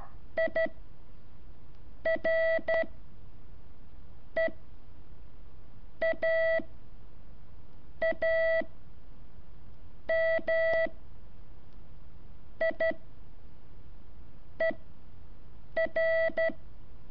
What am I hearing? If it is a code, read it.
Morse code, heard as a keyed tone, about 12 wpm, spaced as for 5 wpm, 650 Hz. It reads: IREAAMIER
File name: curso / ejercicio 5